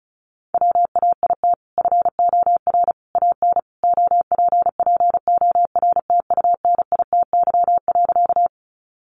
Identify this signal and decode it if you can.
Morse code, heard as a keyed tone, about 34 wpm, 700 Hz. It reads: WAIT FOR AN OPPORTUNITY.
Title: Sentences from Top 1000 Words 35wpm